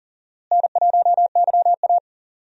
Morse code, heard as a keyed tone, about 40 wpm, 700 Hz. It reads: N1YA
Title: USA Call Signs - 4 Characters - Repeated 3x - 40wpm